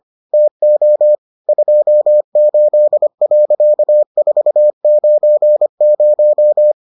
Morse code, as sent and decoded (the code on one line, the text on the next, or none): - --- / ..--- ---.. .-.-.- ....- ----. -----
TO 28.490